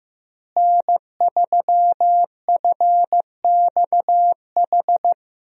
Morse code, heard as a keyed tone, about 14 wpm, 700 Hz. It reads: N3FXH
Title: USA Call Signs - 5 Characters - Repeated 2x - 15wpm